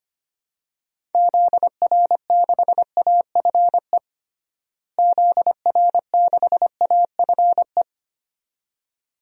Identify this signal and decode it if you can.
Morse code, heard as a keyed tone, about 25 wpm, 700 Hz. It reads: ZR6AFE ZR6AFE
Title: DXCC-100 Call Signs - 6 Characters - Repeated 2x - 25wpm